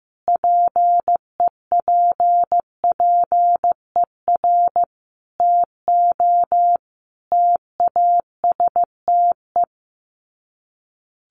Morse code, heard as a keyed tone, about 15 wpm, 700 Hz. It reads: PEPPER TO TASTE
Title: Sets of 5 Words 15wpm